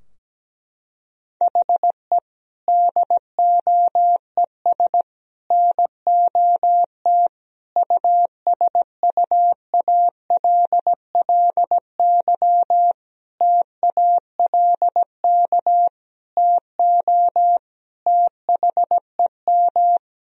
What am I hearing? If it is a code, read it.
Morse code, heard as a keyed tone, about 17 wpm, 700 Hz. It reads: HE DOES NOT USUALLY TALK TO THEM
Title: Sentences from Top 1000 Words - Review Part 3 - 17wpm